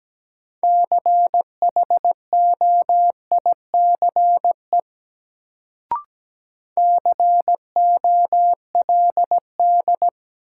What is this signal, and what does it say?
Morse code, heard as a keyed tone, about 17 wpm, 700 Hz. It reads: CHOICE COLD
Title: Sets of 5 Words - New Word Drill - 17wpm